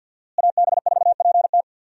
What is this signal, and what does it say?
Morse code, heard as a keyed tone, about 46 wpm, 700 Hz. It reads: AB4PT
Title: USA Call Signs - 5 Characters - Repeated 3x - 50wpm